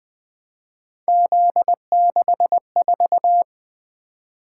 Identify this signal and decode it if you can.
Morse code, heard as a keyed tone, about 21 wpm, 700 Hz. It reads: Z64
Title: Three Letter-Number Combinations - 20wpm